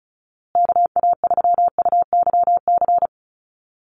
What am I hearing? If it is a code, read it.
Morse code, heard as a keyed tone, about 34 wpm, 700 Hz. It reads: KA3UYC
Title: USA Call Signs Repeated 2x 35wpm